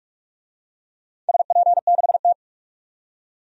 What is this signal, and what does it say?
Morse code, heard as a keyed tone, about 41 wpm, 700 Hz. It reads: SP6T
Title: DXCC-100 Call Signs - 4 Characters - 45wpm